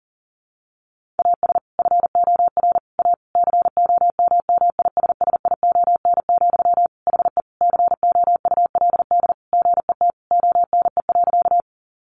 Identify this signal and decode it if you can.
Morse code, heard as a keyed tone, about 40 wpm, 700 Hz. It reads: AS FOR A COMMISSION, HE COULD GET ONE.
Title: Sentences from Top 1400 Words 40wpm